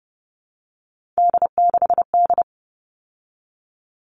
Morse code, heard as a keyed tone, about 30 wpm, 700 Hz. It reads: D6D
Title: All Cut Numbers - 3-Digit 30wpm